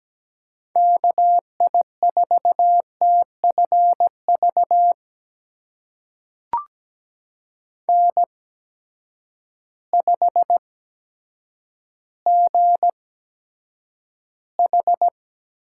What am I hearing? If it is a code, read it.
Morse code, heard as a keyed tone, about 17 wpm, spaced as for 8 wpm, 700 Hz. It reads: KI4TFV N5GH
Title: USA Call Signs - 8x Character Spacing - 17wpm